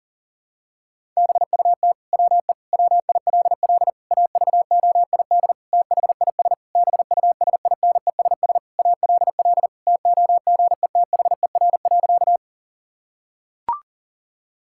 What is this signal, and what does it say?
Morse code, heard as a keyed tone, about 40 wpm, 700 Hz. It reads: BUT WE WILL AVOID THIS BUSINESS ALL TOGETHER.
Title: Sentences from Top 1000 Words - Review Part 6 - 40wpm